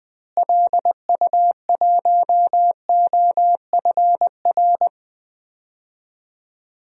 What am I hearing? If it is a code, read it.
Morse code, heard as a keyed tone, about 20 wpm, 700 Hz. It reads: LU1OFR
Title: DXCC-100 Call Signs - 6 Characters - Repeated 3x - 20wpm